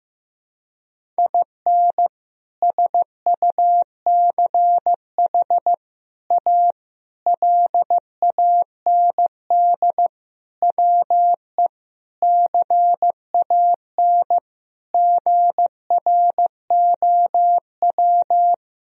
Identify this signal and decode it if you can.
Morse code, heard as a keyed tone, about 15 wpm, 700 Hz. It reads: IN SUCH A LAND WE CAN GROW
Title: Sentences from Top 1000 Words - Review Part 1 - 15wpm